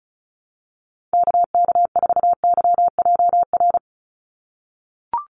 Morse code, heard as a keyed tone, about 36 wpm, 700 Hz. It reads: KK4YJR
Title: USA Call Signs 35wpm